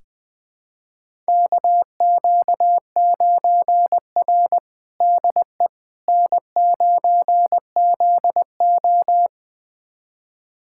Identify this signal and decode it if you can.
Morse code, heard as a keyed tone, about 20 wpm, 700 Hz. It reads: KQ9R DE N9ZO
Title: USA Callsign DE Callsign - 4 Characters - 20wpm